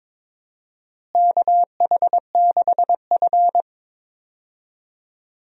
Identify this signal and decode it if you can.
Morse code, heard as a keyed tone, about 22 wpm, 700 Hz. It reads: KH6F
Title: USA Call Signs - 4 Characters - Repeated 2x - 22wpm